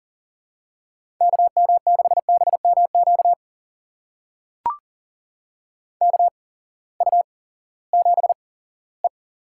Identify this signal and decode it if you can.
Morse code, heard as a keyed tone, about 39 wpm, spaced as for 20 wpm, 700 Hz. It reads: KM6BMQ KU7E
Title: USA Call Signs - 8x Character Spacing - 40wpm